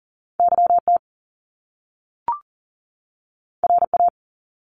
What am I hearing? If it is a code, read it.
Morse code, heard as a keyed tone, about 39 wpm, 700 Hz. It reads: YT RA
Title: Top Two Letter Combinations in Words 40wpm